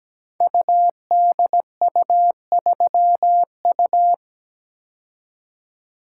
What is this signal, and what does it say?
Morse code, heard as a keyed tone, about 17 wpm, 700 Hz. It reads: UDU3U
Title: Code Groups - 5 Letter:Numbers - 17wpm